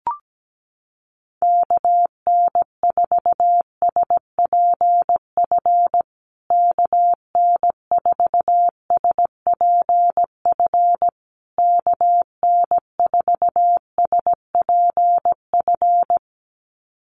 Morse code, heard as a keyed tone, about 17 wpm, 700 Hz. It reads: KN4SPF KN4SPF KN4SPF
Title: USA Call Signs Repeated 3x 17wpm